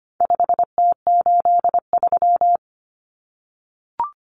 Morse code, heard as a keyed tone, about 25 wpm, 700 Hz. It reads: 5T83
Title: Common Cut Numbers - 4-Digit 25wpm